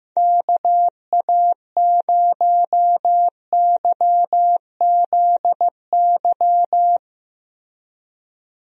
Morse code, heard as a keyed tone, about 15 wpm, 700 Hz. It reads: KA0YZY
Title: USA Call Signs - 6 Characters - Repeated 3x - 15wpm